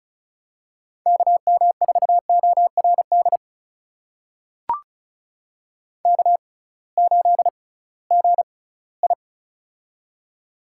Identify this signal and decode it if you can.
Morse code, heard as a keyed tone, about 36 wpm, spaced as for 20 wpm, 700 Hz. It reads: KM4ORD K8GI
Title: USA Call Signs - 6x Character Spacing - 35wpm